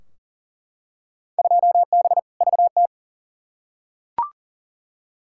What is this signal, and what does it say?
Morse code, heard as a keyed tone, about 39 wpm, 700 Hz. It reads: 2B VT
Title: Field Day Exchanges Encore - 40wpm